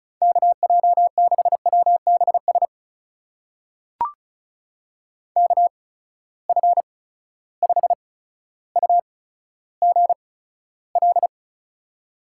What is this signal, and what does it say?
Morse code, heard as a keyed tone, about 36 wpm, spaced as for 15 wpm, 700 Hz. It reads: KJ6WBS KF5UGL
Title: USA Call Signs - 6 Characters - 8x Character Spacing - 35wpm